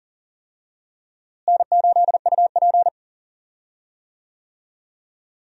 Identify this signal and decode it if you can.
Morse code, heard as a keyed tone, about 40 wpm, 700 Hz. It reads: N8UP
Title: USA Call Signs - 4 Characters - Repeated 3x - 40wpm